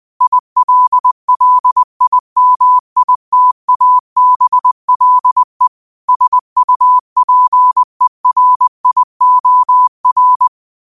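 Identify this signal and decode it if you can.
Morse code, heard as a keyed tone, about 20 wpm, 1000 Hz. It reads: ILLIMITABLE SUPERIOR
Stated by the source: Quote of the day sent in Morse at 20 words per minute.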